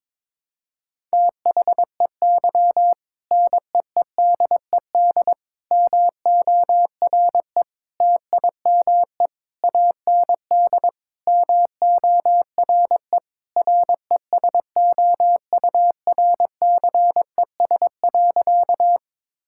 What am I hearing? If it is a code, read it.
Morse code, heard as a keyed tone, about 22 wpm, 700 Hz. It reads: THEY NEEDED MORE TIME AND MORE RESOURCES.
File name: practice / 22wpm / Sentences with 2-Word Phrases - 22wpm